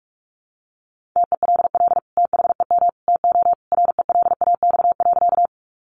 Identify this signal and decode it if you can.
Morse code, heard as a keyed tone, about 44 wpm, 700 Hz. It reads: TELL THEM TO RELAX.
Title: Sentences from Top 1400 Words 45wpm